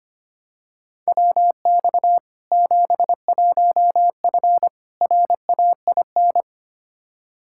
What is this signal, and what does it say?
Morse code, heard as a keyed tone, about 25 wpm, 700 Hz. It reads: WX 71F RAIN
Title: K1USN SST QSO Practice 25wpm